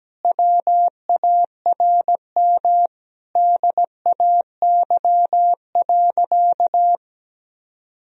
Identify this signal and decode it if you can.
Morse code, heard as a keyed tone, about 17 wpm, 700 Hz. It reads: WARM DAY.
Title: Sentences from Top 1000 Words - Review Part 2 - 17wpm